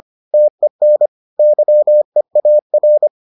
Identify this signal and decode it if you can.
Morse code, heard as a keyed tone, about 25 wpm, 600 Hz. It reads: TEN YEAR